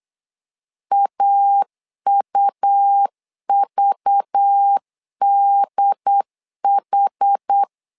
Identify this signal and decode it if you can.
Morse code, heard as a keyed tone, about 8 wpm, 800 Hz. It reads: AUVDH
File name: LEZIONI / 5 LBVC / 5 LBV 8wpm 2837